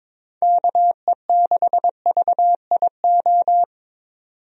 Morse code, heard as a keyed tone, about 22 wpm, 700 Hz. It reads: KE6VIO
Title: USA Call Signs - 6 Characters - 22wpm